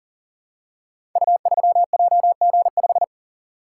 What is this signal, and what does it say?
Morse code, heard as a keyed tone, about 41 wpm, 700 Hz. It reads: U3JG5
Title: Code Groups - 5 Letter:Numbers - 40wpm